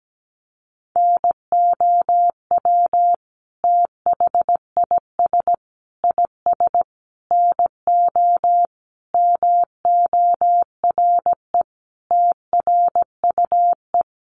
Morse code, heard as a keyed tone, about 17 wpm, 700 Hz. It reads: NOW THIS IS NO MORE TRUE
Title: Sentences from Top 1400 Words 17wpm